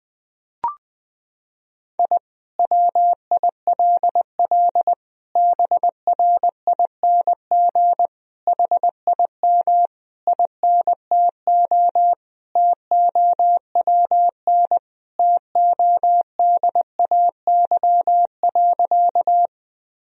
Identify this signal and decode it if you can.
Morse code, heard as a keyed tone, about 20 wpm, 700 Hz. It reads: I WILL BRING HIM INTO TOWN TODAY.
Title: Sentences from Top 1000 Words - Review Part 3 - 20wpm